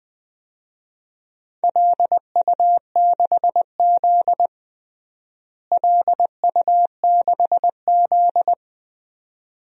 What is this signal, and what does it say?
Morse code, heard as a keyed tone, about 21 wpm, 700 Hz. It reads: LU6Z LU6Z
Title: DXCC-100 Call Signs - 4 Characters - Repeated 2x - 20wpm